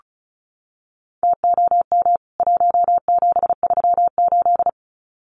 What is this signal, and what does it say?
Morse code, heard as a keyed tone, about 34 wpm, 700 Hz. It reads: TOM 1738
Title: CWT Exchange 35wpm